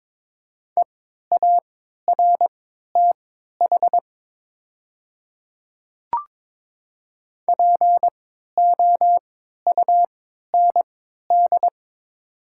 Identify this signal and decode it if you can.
Morse code, heard as a keyed tone, about 22 wpm, spaced as for 12 wpm, 700 Hz. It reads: EARTH POUND
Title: 3x Character Spacing - Top 500 Words - 22wpm